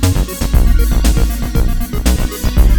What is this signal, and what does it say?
There is no code.